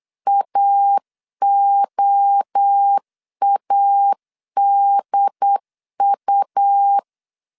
Morse code, heard as a keyed tone, about 8 wpm, 800 Hz. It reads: AOADU